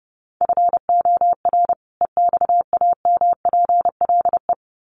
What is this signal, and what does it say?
Morse code, heard as a keyed tone, about 30 wpm, 700 Hz. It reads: FOR EXAMPLE
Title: Sets of 2 Words 30wpm